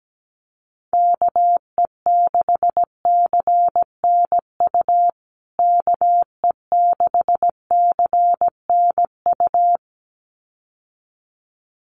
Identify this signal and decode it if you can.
Morse code, heard as a keyed tone, about 17 wpm, 700 Hz. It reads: KE6CNU KE6CNU
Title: USA Call Signs Repeated 2x 17wpm